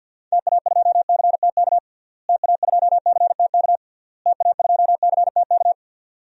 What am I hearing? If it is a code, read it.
Morse code, heard as a keyed tone, about 51 wpm, 700 Hz. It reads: TA2CTK TA2CTK TA2CTK
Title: DXCC-100 Call Signs - 6 Characters - Repeated 3x - 50wpm